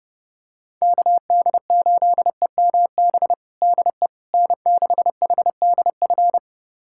Morse code, heard as a keyed tone, about 30 wpm, 700 Hz. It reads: KD8EMB DE N6HDF
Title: USA Callsign DE Callsign 30wpm